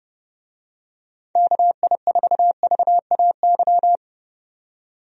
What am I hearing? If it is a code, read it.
Morse code, heard as a keyed tone, about 32 wpm, 700 Hz. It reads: KI4VAY
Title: USA Call Signs - 6 Characters - 30wpm